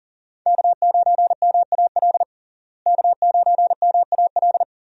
Morse code, heard as a keyed tone, about 40 wpm, 700 Hz. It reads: K9MAL K9MAL
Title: USA Call Signs - 5 Characters - Repeated 3x - 40wpm